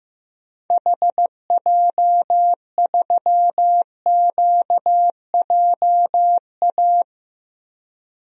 Morse code, heard as a keyed tone, about 15 wpm, 700 Hz. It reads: HJ3QJA
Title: DXCC-100 Call Signs - 6 Characters - Repeated 2x - 15wpm